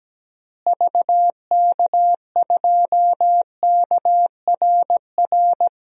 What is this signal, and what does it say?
Morse code, heard as a keyed tone, about 17 wpm, 700 Hz. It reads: VK2KRR
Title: DXCC-100 Call Signs - 6 Characters - Repeated 2x - 17wpm